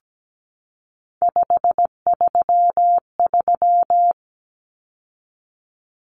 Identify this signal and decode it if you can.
Morse code, heard as a keyed tone, about 17 wpm, 700 Hz. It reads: 533
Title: Numbers - 3-Digit 17wpm